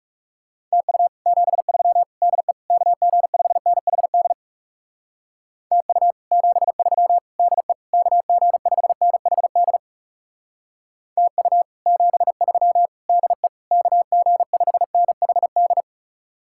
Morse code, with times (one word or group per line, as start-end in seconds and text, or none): 0.7-1.1 TU
1.3-2.0 73
2.2-2.5 DE
2.7-4.3 KG5NHD
5.7-6.1 TU
6.3-7.2 73
7.4-7.7 DE
7.9-9.8 KG5NHD
11.2-11.6 TU
11.9-12.9 73
13.1-13.5 DE
13.7-15.8 KG5NHD